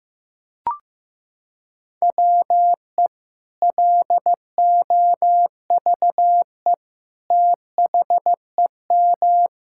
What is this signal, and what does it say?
Morse code, heard as a keyed tone, about 15 wpm, 700 Hz. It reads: WE LOVE THEM
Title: Sentences from Top 1600 Words 15wpm